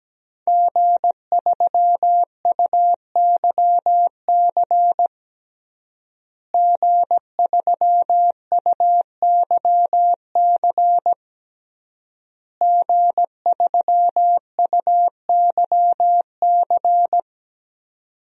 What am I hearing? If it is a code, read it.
Morse code, heard as a keyed tone, about 17 wpm, 700 Hz. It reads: G3UYC G3UYC G3UYC
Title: DXCC-100 Call Signs - 5 Characters - Repeated 3x - 17wpm